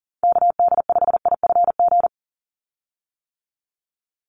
Morse code, heard as a keyed tone, about 40 wpm, 690 Hz. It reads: KD5IFG
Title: USA Call Signs 40wpm